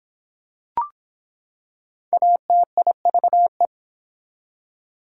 Morse code, heard as a keyed tone, about 24 wpm, 700 Hz. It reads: ATIVE